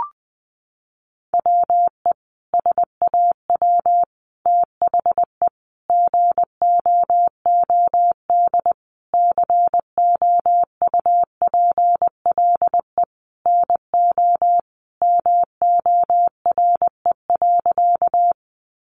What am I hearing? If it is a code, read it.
Morse code, heard as a keyed tone, about 20 wpm, 700 Hz. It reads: WE SAW THE GOOD COUPLE NO MORE.
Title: Sentences from Top 1100 Words 20wpm